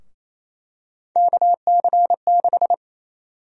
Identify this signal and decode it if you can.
Morse code, heard as a keyed tone, about 27 wpm, 700 Hz. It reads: KC6
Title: USA Call Sign Prefixes - 28wpm